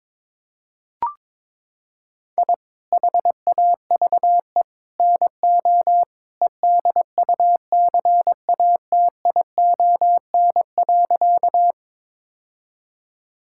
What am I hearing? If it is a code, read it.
Morse code, heard as a keyed tone, about 22 wpm, 700 Hz. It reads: I HAVE NO EDUCATION.
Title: Sentences from Top 1000 Words - Review Part 6 - 22wpm